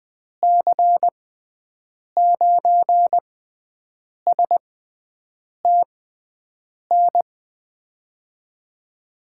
Morse code, heard as a keyed tone, about 20 wpm, spaced as for 7 wpm, 700 Hz. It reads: C9STN
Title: USA Call Signs - 6 Characters - 6x Character Spacing - 20wpm